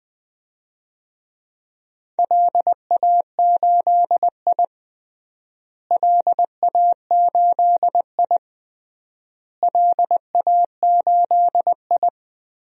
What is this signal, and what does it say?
Morse code, heard as a keyed tone, about 20 wpm, 700 Hz. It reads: LA8I LA8I LA8I
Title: DXCC-100 Call Signs - 4 Characters - Repeated 3x - 20wpm